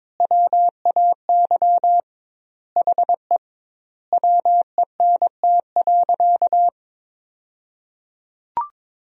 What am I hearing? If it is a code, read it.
Morse code, heard as a keyed tone, about 22 wpm, 700 Hz. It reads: WAY HE WENT.
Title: 2x Word-Spacing - Sentences from Top 200 Words - 22wpm